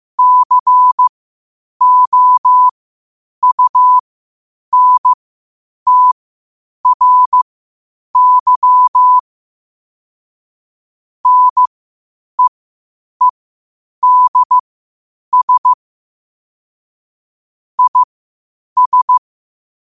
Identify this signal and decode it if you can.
Morse code, heard as a keyed tone, about 15 wpm, spaced as for 8 wpm, 1000 Hz. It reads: COUNTRY NEEDS IS
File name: qotdfiles / 8wpm